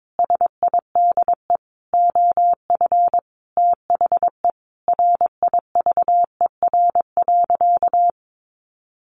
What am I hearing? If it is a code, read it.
Morse code, heard as a keyed tone, about 22 wpm, 700 Hz. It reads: SIDE OF THE RIVER.
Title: Sentences from Top 1300 Words 22wpm